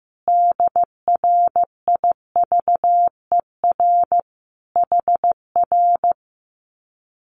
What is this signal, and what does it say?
Morse code, heard as a keyed tone, about 15 wpm, 700 Hz. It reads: DRIVER HR
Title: Common QSO Phrases Encore 15wpm